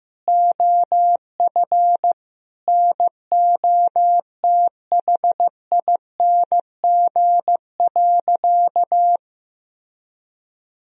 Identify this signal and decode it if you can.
Morse code, heard as a keyed tone, about 15 wpm, 700 Hz. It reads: OF NOTHING.
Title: Sentences from Top 1000 Words - Review Part 3 - 15wpm